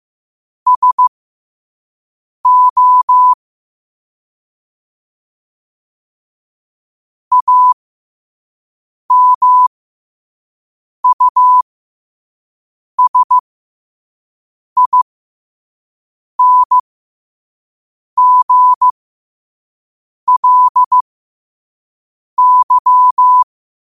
Morse code, heard as a keyed tone, about 15 wpm, spaced as for 5 wpm, 1000 Hz. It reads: SO AMUSINGLY
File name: qotdfiles / 5wpm